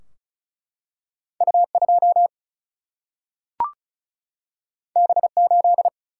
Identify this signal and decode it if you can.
Morse code, heard as a keyed tone, about 36 wpm, 700 Hz. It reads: U2 B8
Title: Single-Letter Single-Number 35wpm